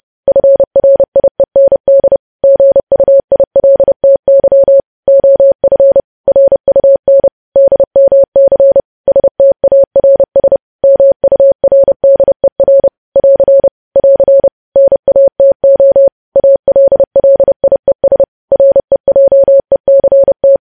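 Morse code, heard as a keyed tone, about 30 wpm, 550 Hz. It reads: FRIEND GUILTY OF RUN DMC STARS MURDER <AR> <AR> NATO ALLIES REJECT
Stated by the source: Morse code news headlines sent at 30, preceded by vvv vvv.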